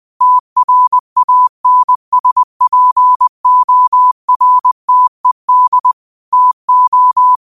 Morse code, heard as a keyed tone, about 20 wpm, 1000 Hz. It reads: TRANSPORTED TO
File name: qotdfiles / 20wpm